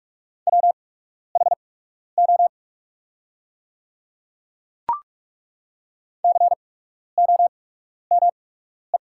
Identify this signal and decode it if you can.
Morse code, heard as a keyed tone, about 42 wpm, spaced as for 12 wpm, 700 Hz. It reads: WHO COME
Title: 8x Character Spacing - Top 100 Words - 45wpm